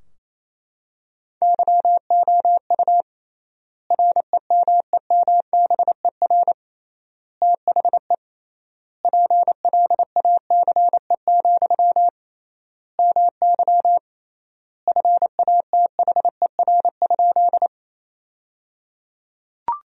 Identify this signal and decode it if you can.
Morse code, heard as a keyed tone, about 28 wpm, 700 Hz. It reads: YOU REMEMBER THE PLACE, MY FATHER?
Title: 3x Word-Spacing - Sentences from Top 400 Words - 28wpm